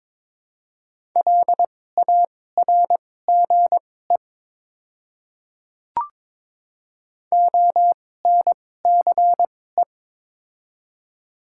Morse code, heard as a keyed tone, about 22 wpm, spaced as for 16 wpm, 700 Hz. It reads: LARGE ONCE
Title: 2x Character Spacing - Top 300 Words - 22wpm